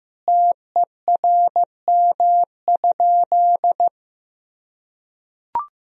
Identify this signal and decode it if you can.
Morse code, heard as a keyed tone, about 15 wpm, 700 Hz. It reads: TERM?
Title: Sentences from Top 1000 Words - Review Part 4 - 15wpm